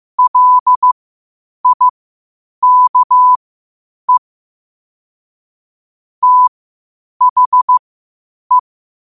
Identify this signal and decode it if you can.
Morse code, heard as a keyed tone, about 15 wpm, spaced as for 8 wpm, 1000 Hz. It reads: LIKE THE
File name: qotdfiles / 8wpm